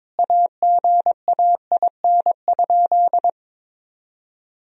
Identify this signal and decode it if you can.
Morse code, heard as a keyed tone, about 22 wpm, 700 Hz. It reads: AGAIN?
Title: Sentences from Top 1000 Words - Review Part 2 - 22wpm